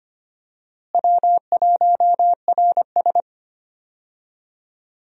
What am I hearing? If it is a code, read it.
Morse code, heard as a keyed tone, about 25 wpm, 700 Hz. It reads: W1RS
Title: USA Call Signs - 4 Characters - Repeated 3x - 25wpm